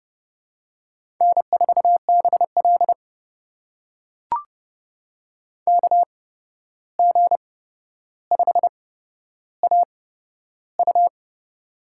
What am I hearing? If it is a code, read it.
Morse code, heard as a keyed tone, about 32 wpm, spaced as for 12 wpm, 700 Hz. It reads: N4BL KG5AU
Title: USA Call Signs - 8x Character Spacing - 30wpm